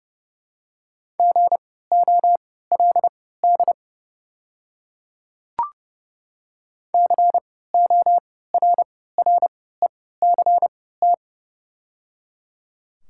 Morse code, heard as a keyed tone, about 30 wpm, spaced as for 17 wpm, 690 Hz. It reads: GOLD CORRECT
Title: 3x Character Spacing - Top 500 Words - 30wpm